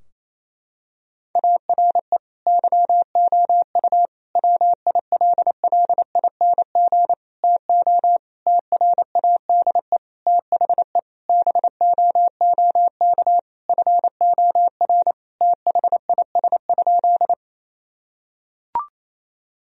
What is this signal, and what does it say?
Morse code, heard as a keyed tone, about 27 wpm, 700 Hz. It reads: ARE YOU WILLING TO TRADE THE BOOK FOR THIS?
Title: Sentences from Top 1000 Words - Review Part 4 - 28wpm